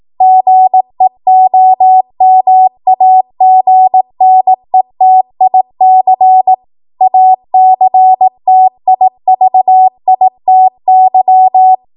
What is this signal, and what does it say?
Morse code, heard as a keyed tone, about 18 wpm, 750 Hz. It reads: GEOMAGNETIC ACTIVITY